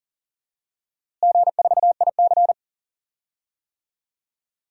Morse code, heard as a keyed tone, about 40 wpm, 700 Hz. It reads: G4IC
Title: DXCC-100 Call Signs - 4 Characters - Repeated 3x - 40wpm